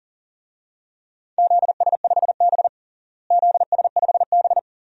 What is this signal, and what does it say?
Morse code, heard as a keyed tone, about 40 wpm, 700 Hz. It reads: ZS5B ZS5B
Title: DXCC-100 Call Signs - 4 Characters - Repeated 2x - 40wpm